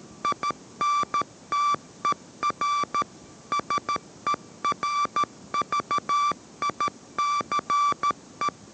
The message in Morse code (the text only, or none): INTER SERVICE